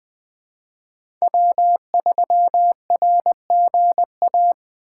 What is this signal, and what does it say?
Morse code, heard as a keyed tone, about 20 wpm, 700 Hz. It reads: W3RGA
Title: K1USN SST QSO Practice - 20wpm